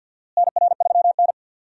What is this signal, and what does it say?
Morse code, heard as a keyed tone, about 52 wpm, 700 Hz. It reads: NR3N
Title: USA Call Signs - 4 Characters - Repeated 2x - 50wpm